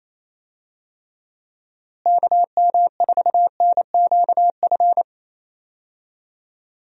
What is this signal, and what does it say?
Morse code, heard as a keyed tone, about 28 wpm, 700 Hz. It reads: KM4NQF
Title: USA Call Signs - 6 Characters - Repeated 3x - 28wpm